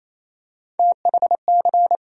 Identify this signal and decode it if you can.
Morse code, heard as a keyed tone, about 28 wpm, 700 Hz. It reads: THC
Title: USA Call Sign Postfixes - 3 Letters 28wpm